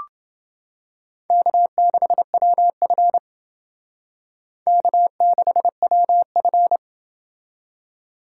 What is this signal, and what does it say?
Morse code, heard as a keyed tone, about 28 wpm, 700 Hz. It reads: K6WF K6WF